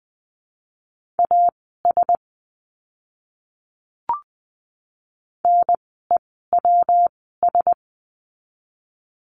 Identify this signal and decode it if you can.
Morse code, heard as a keyed tone, about 20 wpm, spaced as for 15 wpm, 700 Hz. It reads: AS NEWS
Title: 115 Lesson 19f - Letter w - Words 20wpm